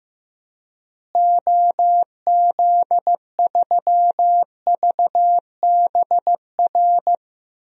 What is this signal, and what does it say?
Morse code, heard as a keyed tone, about 15 wpm, 700 Hz. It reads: OZ3VBR
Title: DXCC-100 Call Signs - 6 Characters - Repeated 2x - 15wpm